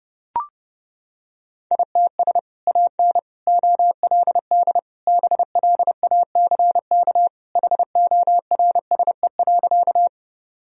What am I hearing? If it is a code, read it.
Morse code, heard as a keyed tone, about 30 wpm, 700 Hz. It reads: ITS AN OLD BLACK HORSE.
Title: Sentences from Top 1000 Words - Review Part 2 - 30wpm